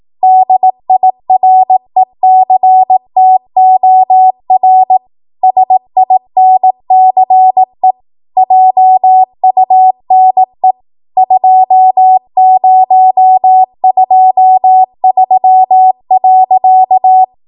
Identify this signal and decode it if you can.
Morse code, heard as a keyed tone, about 18 wpm, 750 Hz. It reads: DIRECTOR SINCE JUNE 2023.
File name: Archive / 18 WPM / 18WPM